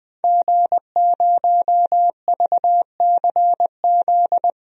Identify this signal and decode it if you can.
Morse code, heard as a keyed tone, about 20 wpm, 700 Hz. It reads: G0VCZ
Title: DXCC-100 Call Signs - 5 Characters - 20wpm